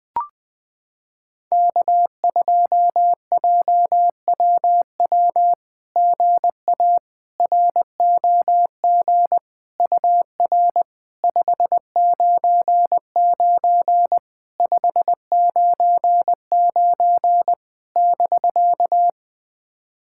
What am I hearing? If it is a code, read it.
Morse code, heard as a keyed tone, about 20 wpm, 700 Hz. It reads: K2JWW GA ROG UR 599 599 <BK>
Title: SOTA QSO Practice 20wpm